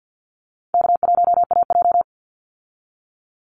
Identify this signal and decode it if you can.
Morse code, heard as a keyed tone, about 49 wpm, 700 Hz. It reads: K1AJ
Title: Easy Call Signs Repeated 2x - 50wpm